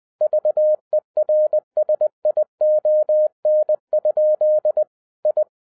Morse code, heard as a keyed tone, about 20 wpm, 610 Hz. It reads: VERSION? I